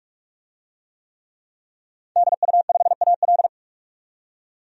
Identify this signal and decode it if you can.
Morse code, heard as a keyed tone, about 46 wpm, 700 Hz. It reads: DU5AL
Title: DXCC-100 Call Signs - 5 Characters - 45wpm